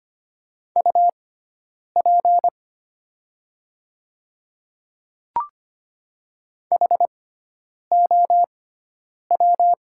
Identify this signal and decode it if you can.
Morse code, heard as a keyed tone, about 25 wpm, spaced as for 9 wpm, 700 Hz. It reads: UP HOW
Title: 6x Character Spacing - Top 100 Words - 25wpm